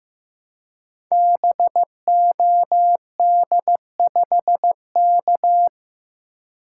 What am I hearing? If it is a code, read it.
Morse code, heard as a keyed tone, about 15 wpm, 700 Hz. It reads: BOD5K
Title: Code Groups - 5 Letter:Numbers - 15wpm